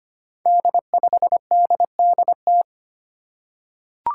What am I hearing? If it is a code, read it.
Morse code, heard as a keyed tone, about 25 wpm, 700 Hz. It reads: D5DDT
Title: Five Challenging Letter-Number Combinations 25wpm